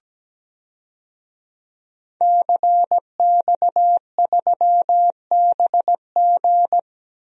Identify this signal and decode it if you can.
Morse code, heard as a keyed tone, about 17 wpm, 700 Hz. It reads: CX3BG
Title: DXCC-100 Call Signs - 5 Characters - Repeated 2x - 17wpm